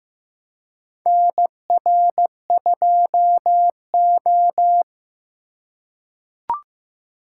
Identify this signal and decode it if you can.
Morse code, heard as a keyed tone, about 15 wpm, 700 Hz. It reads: NR2O
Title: USA Call Signs - 4 Characters - 3x Character Spacing - 15wpm